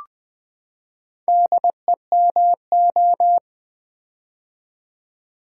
Morse code, heard as a keyed tone, about 20 wpm, 700 Hz. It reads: DEMO
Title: Build A Word - 4 Letters At A Time - 11-Letter Words - 20wpm